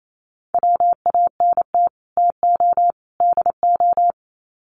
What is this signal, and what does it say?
Morse code, heard as a keyed tone, about 28 wpm, 700 Hz. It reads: WANT TO DO
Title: Sets of 3 Words 28wpm